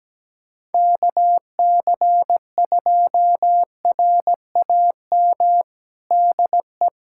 Morse code, heard as a keyed tone, about 17 wpm, 700 Hz. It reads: KC2RAM DE
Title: USA Callsign DE Callsign - 6 Characters - 17wpm